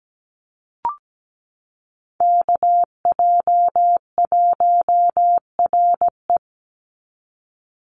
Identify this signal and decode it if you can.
Morse code, heard as a keyed tone, about 17 wpm, 700 Hz. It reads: KJ1RE
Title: Call Signs Repeated 3x - 17wpm